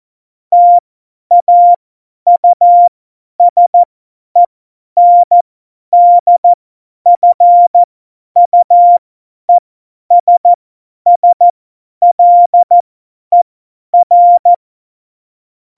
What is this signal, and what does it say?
Morse code, heard as a keyed tone, about 14 wpm, spaced as for 10 wpm, 700 Hz. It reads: TAUSENDFUESSLER
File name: Morsekurs / Witze / Witz 15. Mai 2018, 50 F70